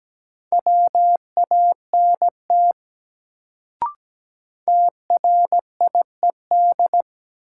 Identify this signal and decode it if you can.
Morse code, heard as a keyed tone, about 17 wpm, 700 Hz. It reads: WANT TRIED
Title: Sets of 2 Words - New Word Drill - 17wpm